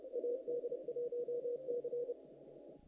Morse code, heard as a keyed tone, about 30 wpm, 480 Hz. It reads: AD1C
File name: contests / naqp-cw-k5tr-op-k5pi- -1900